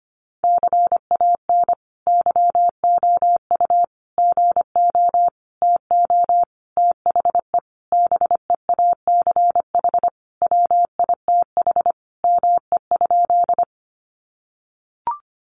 Morse code, heard as a keyed tone, about 25 wpm, 700 Hz. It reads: CAN YOU GO TO THE BEACH WITH ME?
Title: Sentences from Top 1500 Words 25wpm